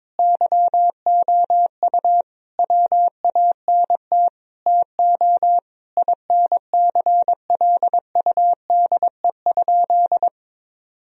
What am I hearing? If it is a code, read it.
Morse code, heard as a keyed tone, about 22 wpm, 700 Hz. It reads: YOU WANT TO INCLUDE?
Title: Sentences from Top 1000 Words - Review Part 3 - 22wpm